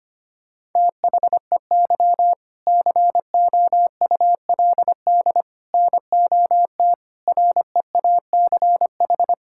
Morse code, heard as a keyed tone, about 25 wpm, 700 Hz. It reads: THEY COULD NOT REACH
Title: Sentences from Top 1000 Words - Review Part 6 - 25wpm